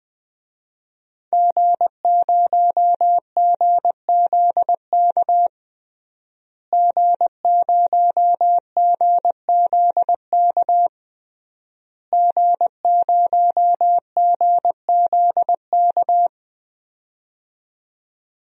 Morse code, heard as a keyed tone, about 20 wpm, 700 Hz. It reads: G0GZK G0GZK G0GZK
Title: DXCC-100 Call Signs - 5 Characters - Repeated 3x - 20wpm